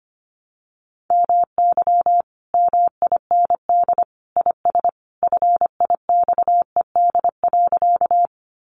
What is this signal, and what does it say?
Morse code, heard as a keyed tone, about 25 wpm, 700 Hz. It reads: MY MIND IS FIXED.
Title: Sentences from Top 1100 Words 25wpm